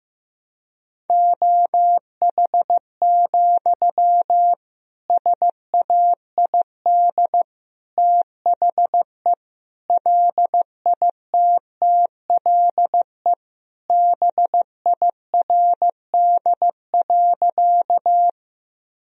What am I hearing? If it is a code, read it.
Morse code, heard as a keyed tone, about 15 wpm, 700 Hz. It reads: OH, SAID THE LITTLE BIRD.
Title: Sentences from Top 1000 Words - Review Part 2 - 15wpm